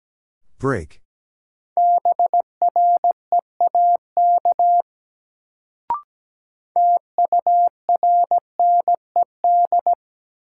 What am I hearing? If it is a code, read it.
Morse code, heard as a keyed tone, about 17 wpm, 700 Hz. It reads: BREAK TURNED
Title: Sets of 4 Words - New Word Drill - 17wpm